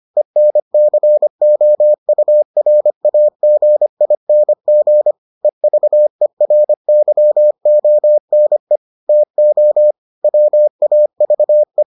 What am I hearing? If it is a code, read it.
Morse code, heard as a keyed tone, about 25 wpm, 600 Hz. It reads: ENCOURAGING EVERYONE TO WAVE